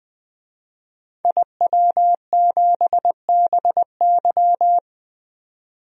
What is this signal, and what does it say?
Morse code, heard as a keyed tone, about 21 wpm, 700 Hz. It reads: IW7BY
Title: DXCC-100 Call Signs - 5 Characters - 20wpm